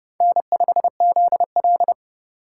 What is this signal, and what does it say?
Morse code, heard as a keyed tone, about 29 wpm, 700 Hz. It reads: N5ZL